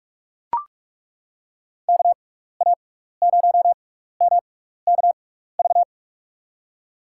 Morse code, heard as a keyed tone, about 41 wpm, spaced as for 15 wpm, 700 Hz. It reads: KA0MKV
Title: USA Call Signs - 6 Characters - 6x Character Spacing - 45wpm